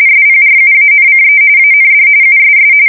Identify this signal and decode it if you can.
RTTY, about 45.45 baud. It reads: SPRUANCE AND USS
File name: fsk170 / USS-New-Jersey